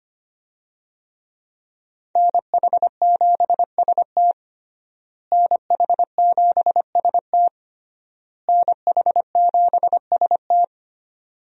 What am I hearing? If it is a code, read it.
Morse code, heard as a keyed tone, about 25 wpm, 700 Hz. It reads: NH7ST NH7ST NH7ST
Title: USA Call Signs - 5 Characters - Repeated 3x - 25wpm